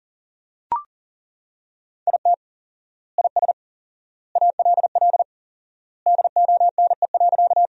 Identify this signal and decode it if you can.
Morse code, heard as a keyed tone, about 40 wpm, 700 Hz. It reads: IT IS ALL DONE.
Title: 4x Word-Spacing - Sentences from Top 500 Words - 40wpm